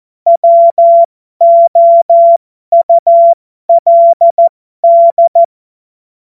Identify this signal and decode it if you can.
Morse code, heard as a keyed tone, about 14 wpm, spaced as for 12 wpm, 680 Hz. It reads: WOULD
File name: Morsekurs / Witze